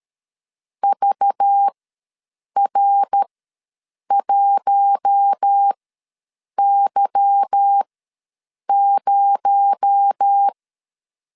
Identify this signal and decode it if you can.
Morse code, heard as a keyed tone, about 13 wpm, spaced as for 7 wpm, 800 Hz. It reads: VR1Y0
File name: LEZIONI / 8 YJ45 / 8 Y ws 12wpm 3409